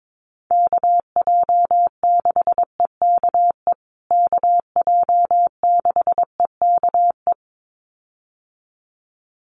Morse code, heard as a keyed tone, about 22 wpm, 700 Hz. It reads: KJ6EKE KJ6EKE
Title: USA Call Signs Repeated 2x 22wpm